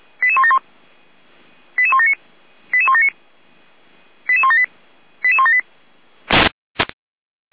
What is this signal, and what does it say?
Five-tone selective call (ZVEI), heard as quick runs of five tones, tones of about 80 ms. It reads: 88171 88179 88179 88178 88178